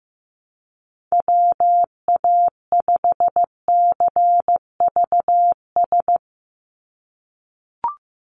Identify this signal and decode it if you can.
Morse code, heard as a keyed tone, about 15 wpm, 700 Hz. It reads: WA5CVS
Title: USA Call Signs 15wpm